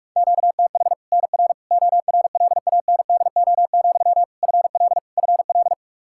Morse code, heard as a keyed tone, about 41 wpm, 700 Hz. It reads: QTH NR ORLANDO, FL FL
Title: Practice QSO - Exchange 1 - Beginning - 45wpm